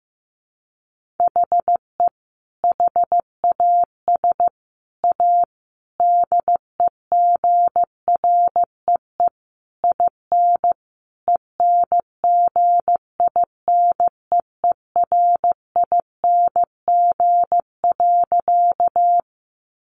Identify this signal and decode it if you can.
Morse code, heard as a keyed tone, about 15 wpm, 700 Hz. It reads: HE HAS A DEGREE IN ENGINEERING.
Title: Sentences from Top 1500 Words 15wpm